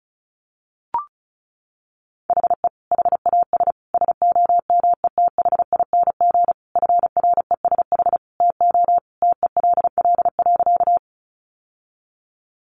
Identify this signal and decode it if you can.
Morse code, heard as a keyed tone, about 34 wpm, 700 Hz. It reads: HE HAS SOMETHING FRESH TO TELL.
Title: Sentences from Top 1200 Words 35wpm